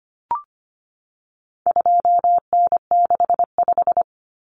Morse code, heard as a keyed tone, about 25 wpm, 700 Hz. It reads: 2N65
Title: Common Cut Numbers - 4-Digit 25wpm